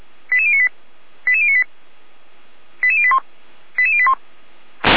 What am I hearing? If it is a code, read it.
Five-tone selective call (ZVEI), heard as quick runs of five tones, tones of about 70 ms. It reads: 88098 88098 88081 88081